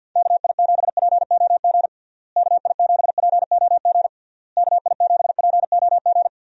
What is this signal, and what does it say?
Morse code, heard as a keyed tone, about 50 wpm, 700 Hz. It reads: KI7POG KI7POG KI7POG
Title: USA Call Signs - 6 Characters - Repeated 3x - 50wpm